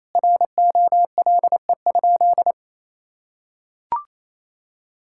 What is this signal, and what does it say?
Morse code, heard as a keyed tone, about 28 wpm, 700 Hz. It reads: ROLE?
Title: Sentences from Top 1000 Words - Review Part 6 - 28wpm